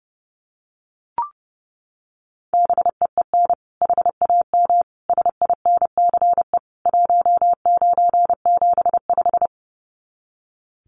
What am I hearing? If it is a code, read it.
Morse code, heard as a keyed tone, about 30 wpm, 700 Hz. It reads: BEEN HAM SINCE 1975
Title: Common QSO Phrases 30wpm